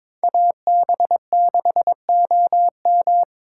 Morse code, heard as a keyed tone, about 21 wpm, 700 Hz. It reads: AB6OM
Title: USA Call Signs - 5 Characters - 3x Character Spacing - 22wpm